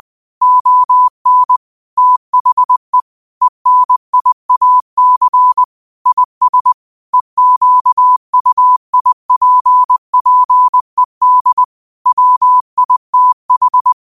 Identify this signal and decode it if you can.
Morse code, heard as a keyed tone, about 20 wpm, 1000 Hz. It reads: ON THE ENIAC IS EQUIPPED WITH